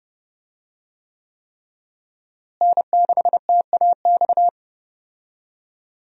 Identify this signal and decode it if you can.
Morse code, heard as a keyed tone, about 30 wpm, 700 Hz. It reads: N6TAX
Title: USA Call Signs - 5 Characters - Repeated 3x - 30wpm